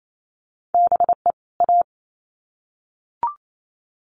Morse code, heard as a keyed tone, about 28 wpm, 700 Hz. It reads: BE A
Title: Sets of 2 Words 28wpm